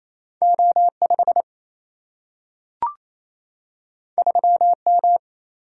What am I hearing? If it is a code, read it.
Morse code, heard as a keyed tone, about 28 wpm, 700 Hz. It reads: O5 3M
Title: Single-Letter Single-Number 28wpm